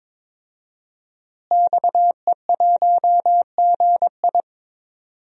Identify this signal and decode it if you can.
Morse code, heard as a keyed tone, about 22 wpm, 700 Hz. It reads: XE1GI
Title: DXCC-100 Call Signs - 5 Characters - Repeated 2x - 22wpm